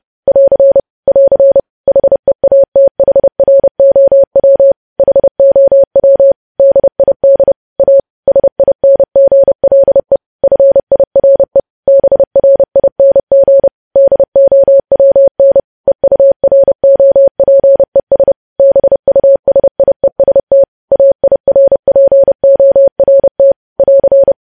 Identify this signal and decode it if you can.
Morse code, heard as a keyed tone, about 30 wpm, 550 Hz. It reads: <AR> <AR> HEATHROW HOW DID A SINGLE FIRE BRING DOWN EUROPES BUSIEST AIRPORT <AR>
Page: news headlines in morse code at 30 wpm